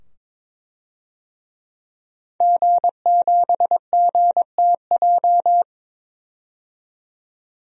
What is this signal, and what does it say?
Morse code, heard as a keyed tone, about 22 wpm, 700 Hz. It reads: G7GTJ
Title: DXCC-100 Call Signs - 5 Characters - 22wpm